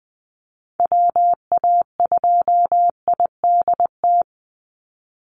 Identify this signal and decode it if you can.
Morse code, heard as a keyed tone, about 20 wpm, 700 Hz. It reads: WA2IDT
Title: USA Call Signs 20wpm